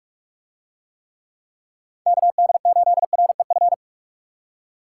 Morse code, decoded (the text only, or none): KD8REF